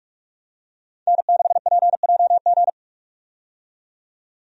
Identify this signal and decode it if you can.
Morse code, heard as a keyed tone, about 44 wpm, 700 Hz. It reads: N6PJG
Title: USA Call Signs - 5 Characters - 45wpm